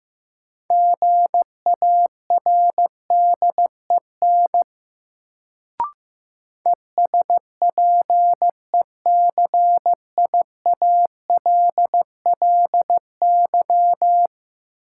Morse code, heard as a keyed tone, about 15 wpm, 700 Hz. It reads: GARDEN ESPECIALLY
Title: Sentences from Top 700 Words - New Word Drill - 15wpm